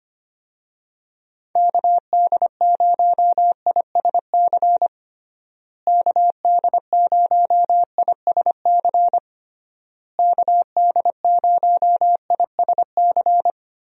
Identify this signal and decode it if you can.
Morse code, heard as a keyed tone, about 25 wpm, 700 Hz. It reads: KD0ISC KD0ISC KD0ISC
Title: USA Call Signs - 6 Characters - Repeated 3x - 25wpm